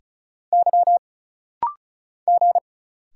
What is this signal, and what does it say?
Morse code, heard as a keyed tone, about 34 wpm, 700 Hz. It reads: Y G
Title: Single Letters - Rapid-Fire 35wpm